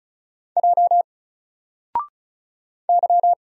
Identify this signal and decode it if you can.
Morse code, heard as a keyed tone, about 36 wpm, 700 Hz. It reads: J Y
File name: Single-Letter-Number-35wpm